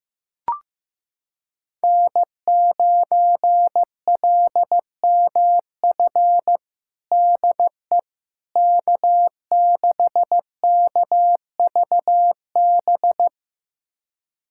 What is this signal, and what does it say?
Morse code, heard as a keyed tone, about 15 wpm, 700 Hz. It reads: N9LMF DE K6KVB
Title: USA Callsign DE Callsign - 5 Characters - 15wpm